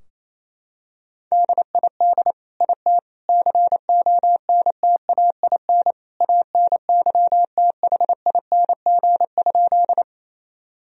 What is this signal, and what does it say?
Morse code, heard as a keyed tone, about 28 wpm, 700 Hz. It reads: DID IT CONTAIN ANYTHING?
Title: Sentences from Top 1000 Words - Review Part 2 - 28wpm